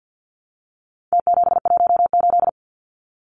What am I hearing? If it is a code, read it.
Morse code, heard as a keyed tone, about 50 wpm, 700 Hz. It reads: T718
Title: Numbers - Contest Style - 4-Digit 50wpm